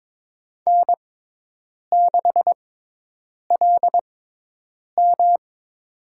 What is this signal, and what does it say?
Morse code, heard as a keyed tone, about 23 wpm, spaced as for 8 wpm, 700 Hz. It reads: N6LM
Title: USA Call Signs - 4 Characters - 6x Character Spacing - 22wpm